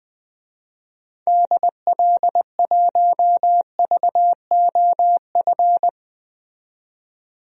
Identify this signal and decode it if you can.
Morse code, heard as a keyed tone, about 21 wpm, 700 Hz. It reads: DL1VOF
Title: DXCC-100 Call Signs - 6 Characters - 20wpm